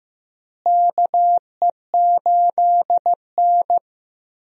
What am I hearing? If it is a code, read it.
Morse code, heard as a keyed tone, about 15 wpm, 700 Hz. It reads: KE8N
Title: USA Call Signs - 4 Characters - Repeated 3x - 15wpm